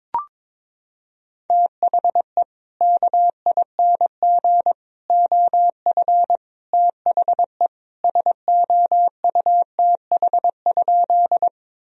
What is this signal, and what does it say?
Morse code, heard as a keyed tone, about 22 wpm, 700 Hz. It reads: THE KING OF THE SOUTH?
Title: Sentences from Top 1000 Words - Review Part 2 - 22wpm